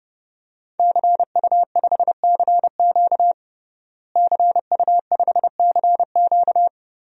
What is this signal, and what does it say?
Morse code, heard as a keyed tone, about 30 wpm, 700 Hz. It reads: CU5CQ CU5CQ
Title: DXCC-100 Call Signs - 5 Characters - Repeated 3x - 30wpm